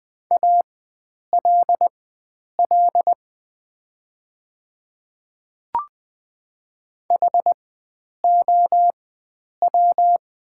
Morse code, heard as a keyed tone, about 20 wpm, spaced as for 9 wpm, 700 Hz. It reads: ALL HOW
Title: 4x Character Spacing - Top 100 Words - 20wpm